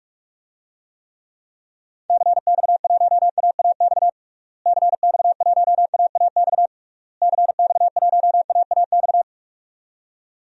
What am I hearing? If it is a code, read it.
Morse code, heard as a keyed tone, about 43 wpm, 700 Hz. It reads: CX1AAX CX1AAX CX1AAX
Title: DXCC-100 Call Signs - 6 Characters - Repeated 3x - 45wpm